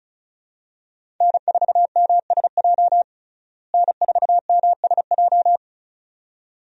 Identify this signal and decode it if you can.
Morse code, heard as a keyed tone, about 36 wpm, 700 Hz. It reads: N4MSJ N4MSJ
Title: USA Call Signs - 5 Characters - Repeated 2x - 35wpm